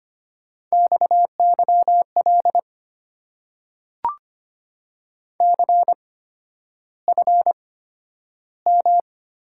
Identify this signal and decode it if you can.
Morse code, heard as a keyed tone, about 25 wpm, spaced as for 11 wpm, 700 Hz. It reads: XYL CFM
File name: Top 90 QSO Elements - 8x Character Spacing - 25wpm